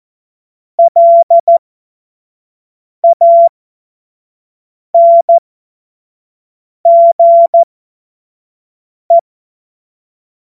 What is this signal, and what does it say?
Morse code, heard as a keyed tone, about 14 wpm, spaced as for 5 wpm, 680 Hz. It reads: LANGE
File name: Morsekurs / Witze / Witz 21. März 2023 25 F70